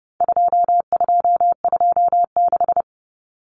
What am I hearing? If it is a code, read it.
Morse code, heard as a keyed tone, about 30 wpm, 700 Hz. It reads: 2226
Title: Common Cut Numbers - 4-Digit 30wpm